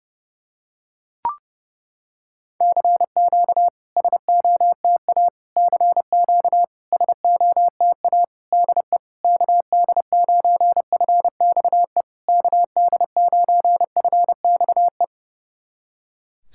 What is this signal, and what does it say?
Morse code, heard as a keyed tone, about 30 wpm, 700 Hz. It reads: CQ SOTA CQ SOTA DE KD9FXE KD9FXE
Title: SOTA QSO Practice 30wpm